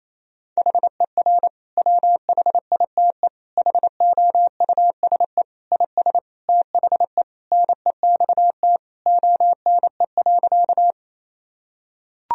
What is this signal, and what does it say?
Morse code, heard as a keyed tone, about 28 wpm, 700 Hz. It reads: HER WHITE HOUSE IS THE NEXT ONE.
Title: Sentences from Top 1000 Words - Review Part 1 - 28wpm